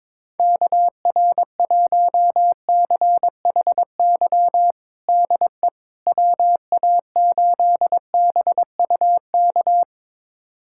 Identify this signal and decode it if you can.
Morse code, heard as a keyed tone, about 22 wpm, 700 Hz. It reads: KR1CHY DE WA8BUK
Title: USA Callsign DE Callsign - 6 Characters - 22wpm